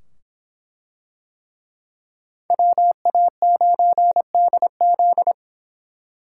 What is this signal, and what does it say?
Morse code, heard as a keyed tone, about 23 wpm, 700 Hz. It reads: WA9DZ